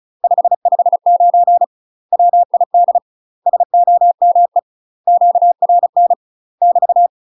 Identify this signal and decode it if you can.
Morse code, heard as a keyed tone, about 36 wpm, 700 Hz. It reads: 559 WID SOME QRN <BT>
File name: UK MASTER QSOs TRAINING / 35WPM / UK MASTER QSOs 0002